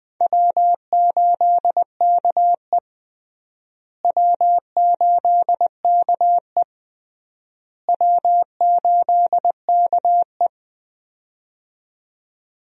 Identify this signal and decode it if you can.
Morse code, heard as a keyed tone, about 21 wpm, 700 Hz. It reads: W8KE W8KE W8KE
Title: USA Call Signs - 4 Characters - Repeated 3x - 20wpm